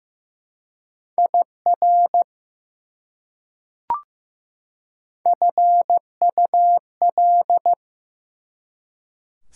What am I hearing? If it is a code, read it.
Morse code, heard as a keyed tone, about 15 wpm, 700 Hz. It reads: IR FUL
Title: Prefixes and Suffixes 15wpm